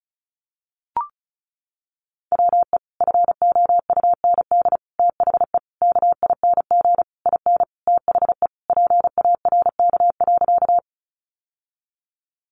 Morse code, heard as a keyed tone, about 34 wpm, 700 Hz. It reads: WE FOUND THE KING IN THE PARK.
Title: Sentences from Top 1300 Words 35wpm